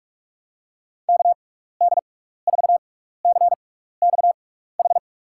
Morse code, heard as a keyed tone, about 46 wpm, 700 Hz. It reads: K D 4 C X H
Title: USA Call Signs - 6 Characters - 6x Character Spacing - 45wpm